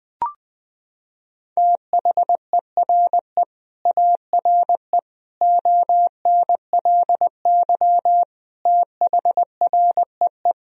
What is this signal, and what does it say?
Morse code, heard as a keyed tone, about 20 wpm, 700 Hz. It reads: THERE ARE ONLY THREE
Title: Sentences from Top 1000 Words - Review Part 1 - 20wpm